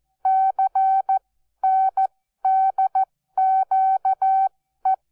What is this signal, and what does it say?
Morse code, heard as a keyed tone, about 14 wpm, spaced as for 12 wpm, 770 Hz. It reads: CNDQE
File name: Onlinelehrgang / morsen / DJ4UF Morsen12